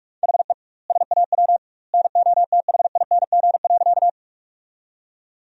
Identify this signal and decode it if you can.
Morse code, heard as a keyed tone, about 41 wpm, 700 Hz. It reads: HE SAW NOTHING.
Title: 2x Word-Spacing - Sentences from Top 500 Words - 45wpm